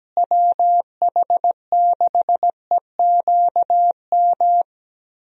Morse code, heard as a keyed tone, about 17 wpm, 700 Hz. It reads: WH6EQM
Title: USA Call Signs - 6 Characters - Repeated 2x - 17wpm